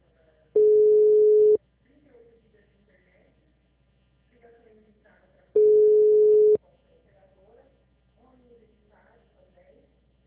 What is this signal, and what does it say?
Ringback tone (Europe, 425 Hz), with speech.